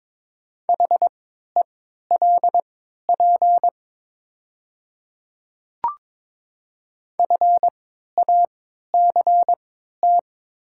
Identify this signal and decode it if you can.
Morse code, heard as a keyed tone, about 22 wpm, spaced as for 12 wpm, 700 Hz. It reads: HELP FACT
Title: 3x Character Spacing - Top 500 Words - 22wpm